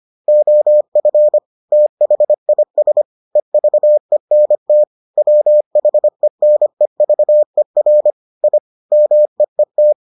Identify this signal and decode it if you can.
Morse code, heard as a keyed tone, about 25 wpm, 600 Hz. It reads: OF THIS EVENT WHENEVER I MEET